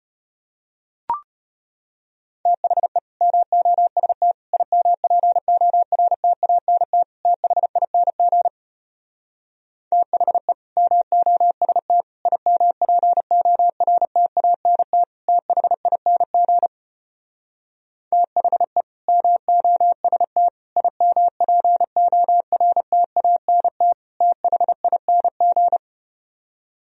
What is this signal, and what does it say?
Morse code, heard as a keyed tone, about 34 wpm, 700 Hz. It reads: THE MOST IMPORTANT THING THE MOST IMPORTANT THING THE MOST IMPORTANT THING